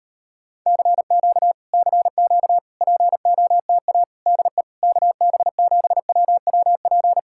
Morse code, heard as a keyed tone, about 35 wpm, 700 Hz. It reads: CQ CQ POTA DE KB7WWP